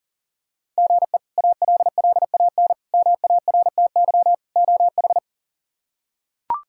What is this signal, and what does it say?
Morse code, heard as a keyed tone, about 40 wpm, 700 Hz. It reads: GE ALLAN MARTY OH
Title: K1USN SST QSO Practice - 40wpm